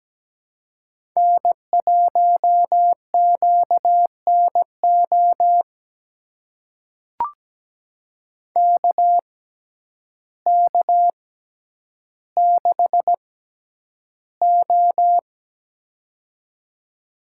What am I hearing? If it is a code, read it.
Morse code, heard as a keyed tone, about 17 wpm, spaced as for 9 wpm, 700 Hz. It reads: N1QNO KK6O
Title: USA Call Signs - 6x Character Spacing - 17wpm